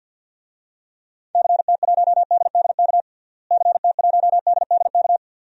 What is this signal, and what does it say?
Morse code, heard as a keyed tone, about 52 wpm, 700 Hz. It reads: CT1DDK CT1DDK
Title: DXCC-100 Call Signs - 6 Characters - Repeated 3x - 50wpm